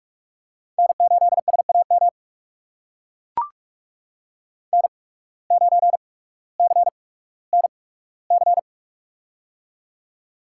Morse code, heard as a keyed tone, about 42 wpm, spaced as for 19 wpm, 700 Hz. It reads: N8SAM N9CNC
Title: USA Call Signs - 5 Characters - 8x Character Spacing - 45wpm